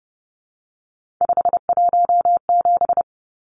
Morse code, heard as a keyed tone, about 30 wpm, 700 Hz. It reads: 517
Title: Numbers - 3-Digit 30wpm